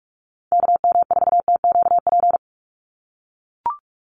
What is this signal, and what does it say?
Morse code, heard as a keyed tone, about 44 wpm, 700 Hz. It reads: KM4TQP
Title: USA Call Signs 45wpm